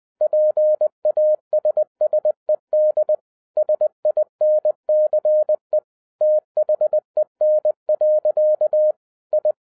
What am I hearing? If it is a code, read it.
Morse code, heard as a keyed tone, about 20 wpm, 610 Hz. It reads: PASSED SINCE THEN. I
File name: 2012-01-mp3 / n16-THROUGH EIGHT MONTHS-20wpm